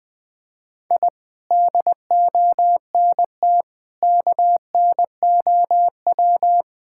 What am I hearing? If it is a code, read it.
Morse code, heard as a keyed tone, about 20 wpm, 700 Hz. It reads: I DONT KNOW
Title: Sentences with 7-Word Phrases - 20wpm